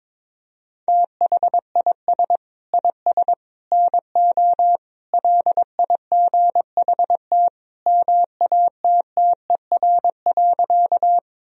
Morse code, heard as a keyed tone, about 22 wpm, 700 Hz. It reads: THIS IS NO LIGHT MATTER.
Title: Sentences from Top 1000 Words - Review Part 1 - 22wpm